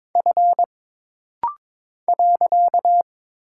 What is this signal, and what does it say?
Morse code, heard as a keyed tone, about 22 wpm, 700 Hz. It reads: F .
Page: This rapid-fire timing is geared to push your ICR (Instant Character Recognition) skills.